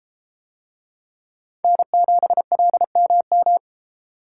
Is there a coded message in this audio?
Yes: Morse code from 1.6 to 3.6 s, heard as a keyed tone, about 32 wpm, 700 Hz. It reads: N7LMM